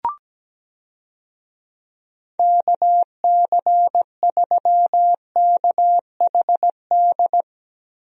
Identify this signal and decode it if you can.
Morse code, heard as a keyed tone, about 17 wpm, 700 Hz. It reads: KC3KHD
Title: USA Call Signs - 6 Characters - Repeated 2x - 17wpm